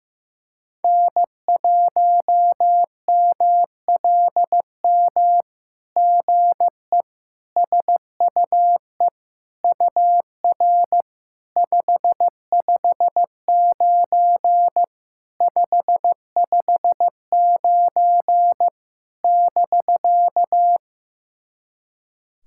Morse code, heard as a keyed tone, about 15 wpm, 700 Hz. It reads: N1MLM GE SUE UR 559 559 <BK>
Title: SOTA QSO Practice 15wpm